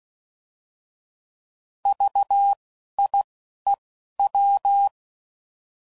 Morse code, heard as a keyed tone, about 16 wpm, 800 Hz. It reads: V I E W
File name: Prefixes / By First Letter / Vs